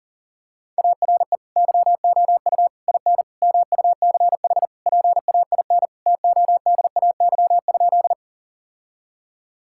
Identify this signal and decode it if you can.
Morse code, heard as a keyed tone, about 37 wpm, 700 Hz. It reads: ARE YOU IN MUCH PAIN TODAY?
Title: Sentences from Top 1600 Words 40wpm